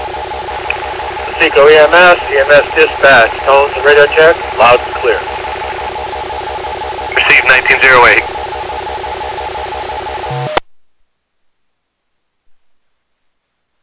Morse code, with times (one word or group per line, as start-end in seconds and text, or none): none